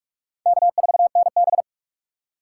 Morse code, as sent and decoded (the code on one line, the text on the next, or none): -.- ....- -. -...
K4NB